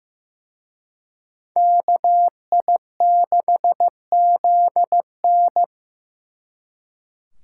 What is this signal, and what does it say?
Morse code, heard as a keyed tone, about 18 wpm, 700 Hz. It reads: KI6ZN
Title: Speed Racing - USA 2x2 Call Signs 15wpm